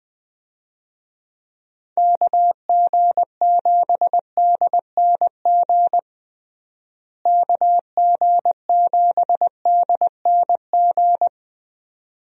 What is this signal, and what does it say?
Morse code, heard as a keyed tone, about 21 wpm, 700 Hz. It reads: KG7DNG KG7DNG
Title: USA Call Signs - 6 Characters - Repeated 2x - 20wpm